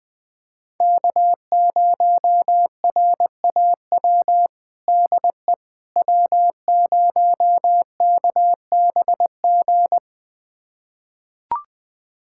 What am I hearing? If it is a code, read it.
Morse code, heard as a keyed tone, about 21 wpm, 700 Hz. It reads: K0RAW DE W0KBG
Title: USA Callsign DE Callsign - 5 Characters - 20wpm